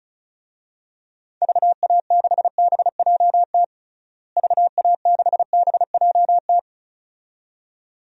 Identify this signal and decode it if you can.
Morse code, heard as a keyed tone, about 36 wpm, 700 Hz. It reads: VA6BJT VA6BJT
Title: DXCC-100 Call Signs - 6 Characters - Repeated 2x - 35wpm